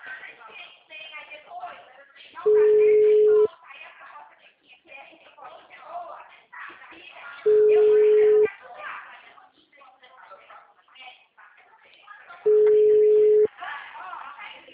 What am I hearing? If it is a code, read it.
Ringback tone (Europe, 425 Hz), with speech.
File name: out- -2006- - - .2705